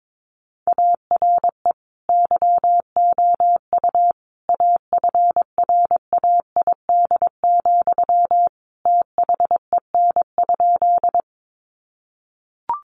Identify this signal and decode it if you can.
Morse code, heard as a keyed tone, about 22 wpm, 700 Hz. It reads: ARE YOU AFRAID, THEN?
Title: Sentences from Top 1500 Words 22wpm